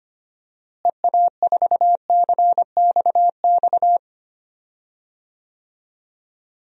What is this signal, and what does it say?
Morse code, heard as a keyed tone, about 25 wpm, 700 Hz. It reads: EA4CXX
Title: DXCC-100 Call Signs - 6 Characters - Repeated 3x - 25wpm